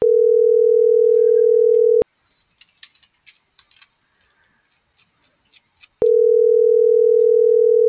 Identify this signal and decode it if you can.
Ringback tone (North America), 440 + 480 Hz.